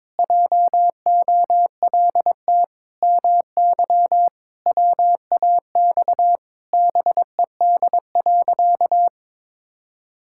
Morse code, heard as a keyed tone, about 22 wpm, 700 Hz. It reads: JOLT MY WAX BED.
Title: Pangram Challenge v2 - 22wpm